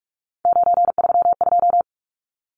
Morse code, heard as a keyed tone, about 44 wpm, 700 Hz. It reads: 932
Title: Numbers - 3-Digit 45wpm